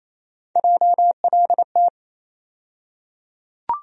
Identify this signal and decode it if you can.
Morse code, heard as a keyed tone, about 28 wpm, 700 Hz. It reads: JLT
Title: USA Call Sign Postfixes - 3 Letters 28wpm